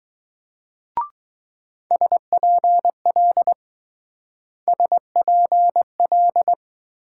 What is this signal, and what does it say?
Morse code, heard as a keyed tone, about 20 wpm, 700 Hz. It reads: SPL SPL